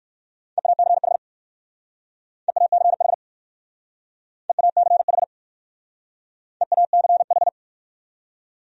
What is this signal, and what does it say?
Morse code, heard as a keyed tone, about 62 wpm, 700 Hz. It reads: EACH EACH EACH EACH